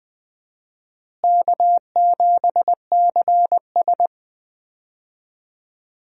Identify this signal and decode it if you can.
Morse code, heard as a keyed tone, about 20 wpm, 700 Hz. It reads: K7CS
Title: USA Call Signs - 4 Characters - Repeated 2x - 20wpm